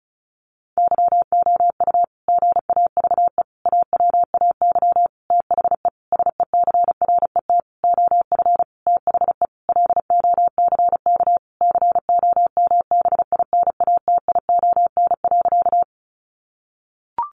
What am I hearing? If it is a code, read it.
Morse code, heard as a keyed tone, about 34 wpm, 700 Hz. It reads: YOU GAVE AWAY THE SECRET OF THE LOCK COMBINATION.
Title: Sentences from Top 1300 Words 35wpm